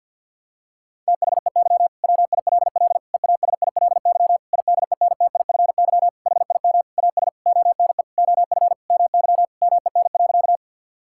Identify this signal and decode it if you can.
Morse code, heard as a keyed tone, about 52 wpm, 700 Hz. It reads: THEY WILL EASILY IDENTIFY HIM AS ONE OF MY MEN.
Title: Sentences from Top 1000 Words - Review Part 6 - 50wpm